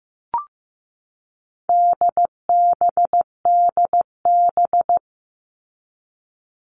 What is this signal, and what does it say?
Morse code, heard as a keyed tone, about 15 wpm, 700 Hz. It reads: DBDB
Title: All Cut Numbers - 4-Digit 15wpm